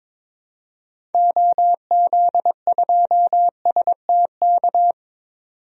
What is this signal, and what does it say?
Morse code, heard as a keyed tone, about 21 wpm, 700 Hz. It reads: OZ2STK
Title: DXCC-100 Call Signs - 6 Characters - 22wpm